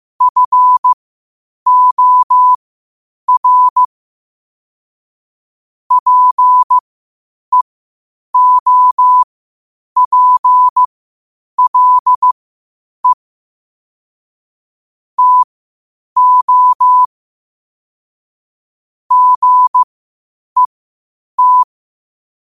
Morse code, heard as a keyed tone, about 15 wpm, spaced as for 8 wpm, 1000 Hz. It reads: FOR PEOPLE TO GET
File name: qotdfiles / 8wpm